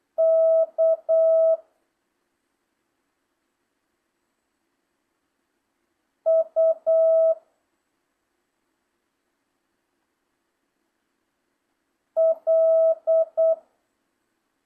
Morse code, heard as a keyed tone, about 8 wpm, 660 Hz. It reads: K U L